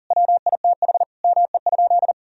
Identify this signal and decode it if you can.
Morse code, heard as a keyed tone, about 40 wpm, 700 Hz. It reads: WITH ME?
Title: Sentences from Top 1000 Words - Review Part 1 - 40wpm